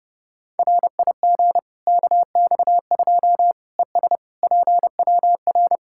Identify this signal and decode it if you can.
Morse code, heard as a keyed tone, about 30 wpm, 700 Hz. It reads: RIG KX2 ES PWR
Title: K1USN SST QSO Practice 30wpm